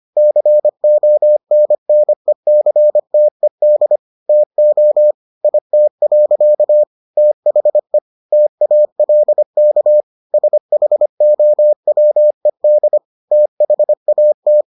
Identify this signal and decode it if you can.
Morse code, heard as a keyed tone, about 25 wpm, 600 Hz. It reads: CONNECTED TO IT. THE TALK SHOWED THAT